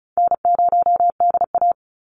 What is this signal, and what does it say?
Morse code, heard as a keyed tone, about 36 wpm, 700 Hz. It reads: N0DA
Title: NA SOTA Chaser Callsigns 35wpm